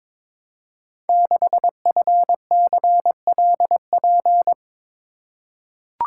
Morse code, heard as a keyed tone, about 22 wpm, 690 Hz. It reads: 6FCLP
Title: Code Groups - 5 Letter:Numbers - 22wpm